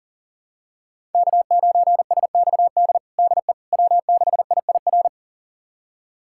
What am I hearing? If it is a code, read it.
Morse code, heard as a keyed tone, about 37 wpm, 690 Hz. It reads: K9SXD DE W6IIR
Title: USA Callsign DE Callsign - 5 Characters - 40wpm